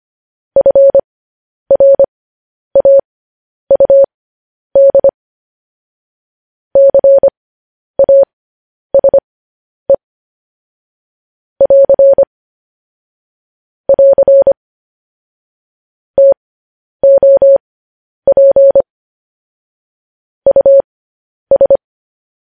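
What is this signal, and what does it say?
Morse code, heard as a keyed tone, about 25 wpm, spaced as for 10 wpm, 550 Hz. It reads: FRAUD CASE <AR> <AR> TOP US
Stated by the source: Morse code transcription of news headlines.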